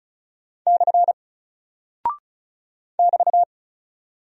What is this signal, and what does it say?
Morse code, heard as a keyed tone, about 35 wpm, 700 Hz. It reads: / <BT>
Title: Single Character (Letter:Number:Punctuation:Prosign) 35wpm